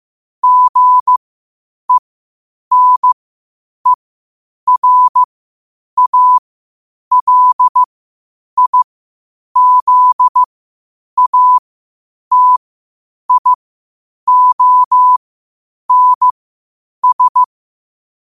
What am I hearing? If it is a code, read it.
Morse code, heard as a keyed tone, about 15 wpm, spaced as for 8 wpm, 1000 Hz. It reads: GENERALIZATIONS
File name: qotdfiles / 8wpm